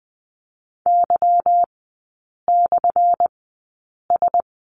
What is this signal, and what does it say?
Morse code, heard as a keyed tone, about 20 wpm, spaced as for 8 wpm, 690 Hz. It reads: Y/S
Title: 212 Lesson 34e - Number 8 - Three Characters 20wpm